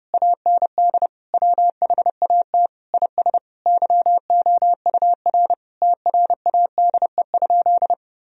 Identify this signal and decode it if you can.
Morse code, heard as a keyed tone, about 30 wpm, 700 Hz. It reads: AND WHAT IS YOUR TRADE?
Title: Sentences from Top 1000 Words - Review Part 4 - 30wpm